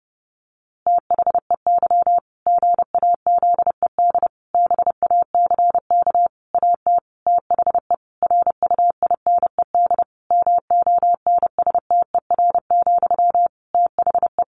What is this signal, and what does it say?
Morse code, heard as a keyed tone, about 30 wpm, 700 Hz. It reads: THEY GAZED BACK AT THE RUINED MONSTER, THE
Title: A Sound Of Thunder 30wpm